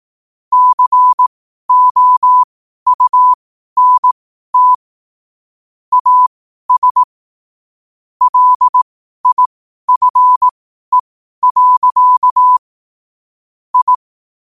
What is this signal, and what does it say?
Morse code, heard as a keyed tone, about 18 wpm, spaced as for 12 wpm, 1000 Hz. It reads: COUNT AS LIFE. I
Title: 12 WPM morse code quotes for Fri, 01 Mar 2024